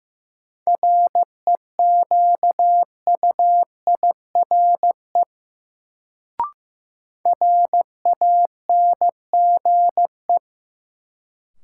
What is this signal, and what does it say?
Morse code, heard as a keyed tone, about 15 wpm, 700 Hz. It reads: REQUIRE RANGE
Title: Sentences from Top 900 Words - New Word Drill - 15wpm